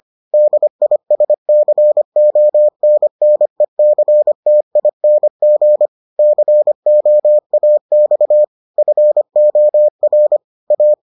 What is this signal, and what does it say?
Morse code, heard as a keyed tone, about 25 wpm, 600 Hz. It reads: DISCONNECTING COAX FOR A